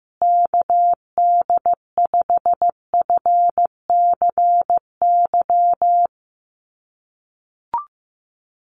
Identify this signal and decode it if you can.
Morse code, heard as a keyed tone, about 15 wpm, 700 Hz. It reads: KD5FCY
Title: USA Call Signs Repeated 2x 15wpm